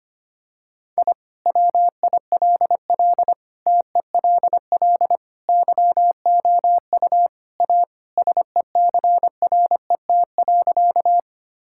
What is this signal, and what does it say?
Morse code, heard as a keyed tone, about 25 wpm, 690 Hz. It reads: I WILL TELL YOU A SECRET.
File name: practice / 25wpm / Sentences with 2-Word Phrases - 25wpm